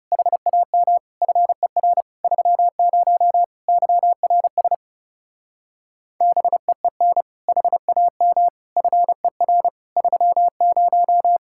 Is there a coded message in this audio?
Yes: Morse code from 0.1 to 11.4 s, heard as a keyed tone, about 40 wpm, 700 Hz. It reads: HAM FER 30 YRS BEEN HAM FER 30